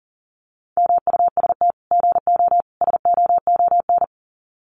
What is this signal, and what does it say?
Morse code, heard as a keyed tone, about 40 wpm, 700 Hz. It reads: MUST GO SOON
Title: Common QSO Phrases 40wpm